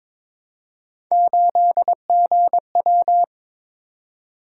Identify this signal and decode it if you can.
Morse code, heard as a keyed tone, about 22 wpm, 700 Hz. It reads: 8GW
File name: practice / 22wpm / Three Letter-Number Combinations - 22wpm